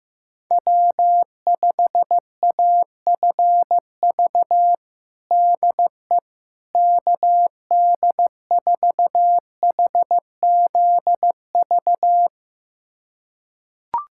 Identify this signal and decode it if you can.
Morse code, heard as a keyed tone, about 15 wpm, 700 Hz. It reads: W5AFV DE KD4HZV
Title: USA Callsign DE Callsign 15wpm